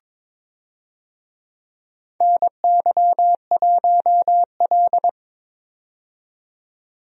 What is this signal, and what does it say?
Morse code, heard as a keyed tone, about 22 wpm, 700 Hz. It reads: NY1L
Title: USA Call Signs - 4 Characters - 22wpm